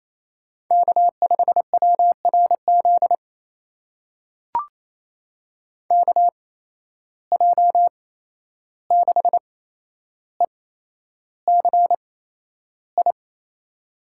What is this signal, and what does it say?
Morse code, heard as a keyed tone, about 27 wpm, spaced as for 11 wpm, 700 Hz. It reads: K5WRZ KJ6ECI
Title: USA Call Signs - 8x Character Spacing - 28wpm